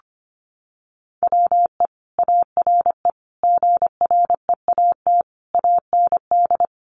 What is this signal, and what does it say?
Morse code, heard as a keyed tone, about 25 wpm, 700 Hz. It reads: WE ARE GREAT AND
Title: Sentences from Top 1300 Words 25wpm